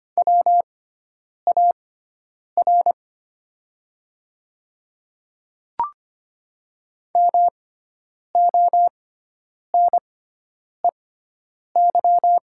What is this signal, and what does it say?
Morse code, heard as a keyed tone, about 25 wpm, spaced as for 9 wpm, 700 Hz. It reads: WAR MONEY
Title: 6x Character Spacing - Top 500 Words - 25wpm